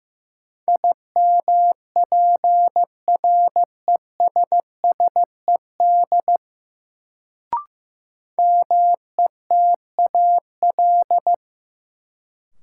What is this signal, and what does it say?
Morse code, heard as a keyed tone, about 15 wpm, 700 Hz. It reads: IMPRESSED METAL
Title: Sentences from Top 1600 Words - New Word Drill Alt - 15wpm